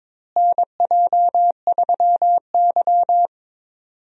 Morse code, heard as a keyed tone, about 21 wpm, 700 Hz. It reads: NJ3Y
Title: USA Call Signs - 4 Characters - Repeated 3x - 22wpm